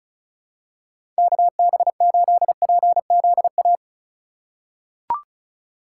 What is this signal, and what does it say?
Morse code, heard as a keyed tone, about 34 wpm, 700 Hz. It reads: KB8PZA
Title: USA Call Signs - 6 Characters - 4x Character Spacing - 35wpm